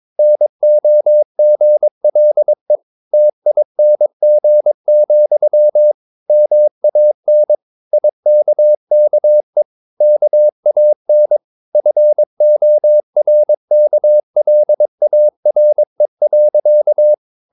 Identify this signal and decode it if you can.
Morse code, heard as a keyed tone, about 22 wpm, 600 Hz. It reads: NOGLE TING, MAN IKKE KAN FORKLARE.
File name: Ferie QRS 22wpm